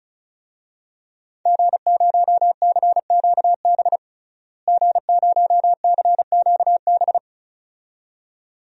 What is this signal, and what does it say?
Morse code, heard as a keyed tone, about 36 wpm, 700 Hz. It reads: G0CQB G0CQB
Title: DXCC-100 Call Signs - 5 Characters - Repeated 2x - 35wpm